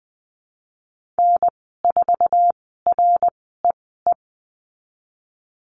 Morse code, heard as a keyed tone, about 20 wpm, spaced as for 15 wpm, 700 Hz. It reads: N4REE
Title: 062 Lesson 11g - Letter h - Call Signs 20wpm